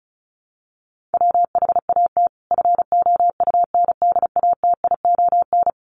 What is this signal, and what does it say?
Morse code, heard as a keyed tone, about 34 wpm, 700 Hz. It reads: WHAT FOUNDATION
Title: Sentences from Top 1200 Words 35wpm